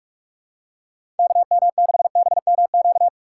Morse code, heard as a keyed tone, about 44 wpm, 700 Hz. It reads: KM6BMQ
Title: USA Call Signs - 6 Characters - 6x Character Spacing - 45wpm